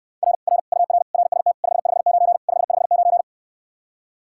Morse code, heard as a keyed tone, about 69 wpm, 700 Hz. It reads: R R UR RST 559 559